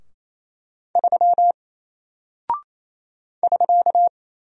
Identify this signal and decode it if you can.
Morse code, heard as a keyed tone, about 27 wpm, 700 Hz. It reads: 3 <SK>
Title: Single Character (Letter:Number:Punctuation:Prosign) 28wpm